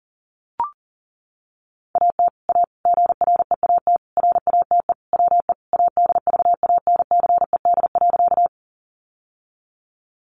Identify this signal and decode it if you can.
Morse code, heard as a keyed tone, about 40 wpm, 700 Hz. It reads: AT A GREAT RATE WE ADVANCED.
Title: Sentences from Top 1200 Words 40wpm